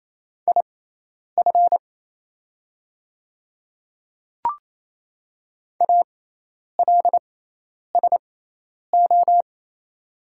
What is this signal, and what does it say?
Morse code, heard as a keyed tone, about 27 wpm, spaced as for 10 wpm, 700 Hz. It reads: IF ALSO
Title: 6x Character Spacing - Top 200 Words - 28wpm